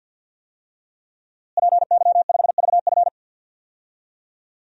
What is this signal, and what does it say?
Morse code, heard as a keyed tone, about 53 wpm, 700 Hz. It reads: PY5VF
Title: DXCC-100 Call Signs - 5 Characters - Repeated 3x - 50wpm